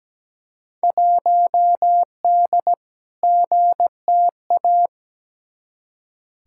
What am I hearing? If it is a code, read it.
Morse code, heard as a keyed tone, about 17 wpm, 700 Hz. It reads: 1D GTA
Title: Field Day Exchanges - 17wpm